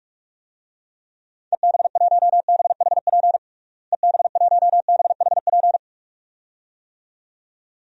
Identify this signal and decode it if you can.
Morse code, heard as a keyed tone, about 41 wpm, 700 Hz. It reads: EB1BHP EB1BHP
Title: DXCC-100 Call Signs - 6 Characters - Repeated 2x - 45wpm